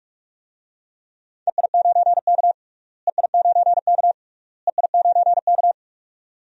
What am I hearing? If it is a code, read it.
Morse code, heard as a keyed tone, about 43 wpm, 700 Hz. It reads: EI9K EI9K EI9K
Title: DXCC-100 Call Signs - 4 Characters - Repeated 3x - 45wpm